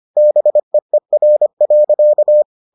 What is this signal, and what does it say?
Morse code, heard as a keyed tone, about 25 wpm, 600 Hz. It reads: BEER.